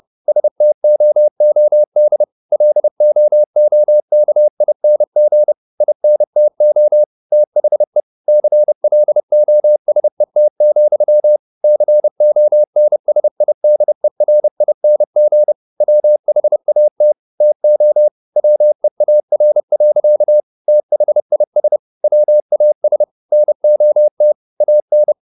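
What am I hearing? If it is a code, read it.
Morse code, heard as a keyed tone, about 30 wpm, 600 Hz. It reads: STOOD LOOKING INTO THE CLOSET, CONSIDERING WHAT TO WEAR. THIS WAS NOT AN
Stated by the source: Morse code at 30 wpm, the novel Never by Ken Follett in English, from CW med Gnister.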